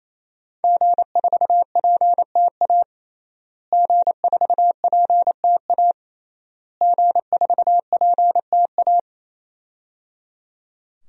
Morse code, heard as a keyed tone, about 28 wpm, 700 Hz. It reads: G4PTA G4PTA G4PTA
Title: DXCC-100 Call Signs - 5 Characters - Repeated 3x - 28wpm